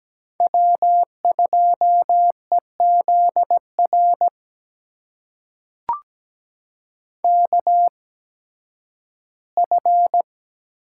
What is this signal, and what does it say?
Morse code, heard as a keyed tone, about 17 wpm, spaced as for 11 wpm, 700 Hz. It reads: W2EZR KF
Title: USA Call Signs - 5 Characters - 8x Character Spacing - 17wpm